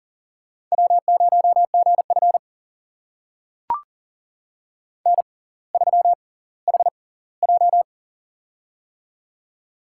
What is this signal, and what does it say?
Morse code, heard as a keyed tone, about 39 wpm, spaced as for 20 wpm, 700 Hz. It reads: W0GF N3HJ
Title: USA Call Signs - 4 Characters - 6x Character Spacing - 40wpm